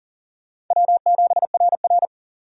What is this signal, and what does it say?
Morse code, heard as a keyed tone, about 40 wpm, 700 Hz. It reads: W7RR
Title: USA Call Signs - 4 Characters - 40wpm